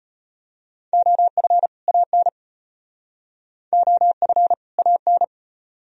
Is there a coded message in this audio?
Yes: Morse code from 0.9 to 5.2 s, heard as a keyed tone, about 35 wpm, 700 Hz. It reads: OF AN OF AN